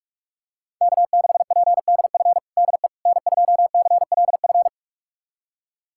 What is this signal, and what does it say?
Morse code, heard as a keyed tone, about 46 wpm, 700 Hz. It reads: K6PDF DE N2CLF
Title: USA Callsign DE Callsign - 5 Characters - 45wpm